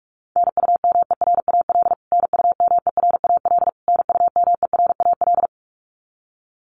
Morse code, heard as a keyed tone, about 44 wpm, 700 Hz. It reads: NUMERAL NUMERAL NUMERAL
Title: Top 500 Words Repeated 45wpm